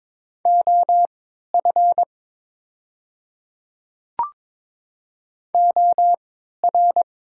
Morse code, heard as a keyed tone, about 23 wpm, spaced as for 13 wpm, 700 Hz. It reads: OF OR
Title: 3x Character Spacing - Top 500 Words - 22wpm